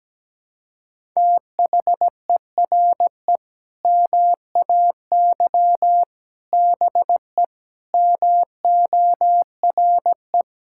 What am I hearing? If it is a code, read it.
Morse code, heard as a keyed tone, about 17 wpm, 700 Hz. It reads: THERE MAY BE MORE
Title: Sentences from Top 1000 Words - Review Part 6 - 17wpm